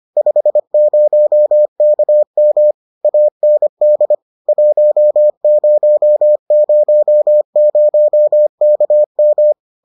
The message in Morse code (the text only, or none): 50KM AND 1000KM